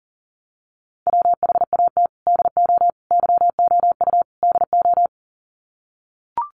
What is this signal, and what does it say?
Morse code, heard as a keyed tone, about 40 wpm, 700 Hz. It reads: WHAT DO YOU DO
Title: Sets of 4 Words 40wpm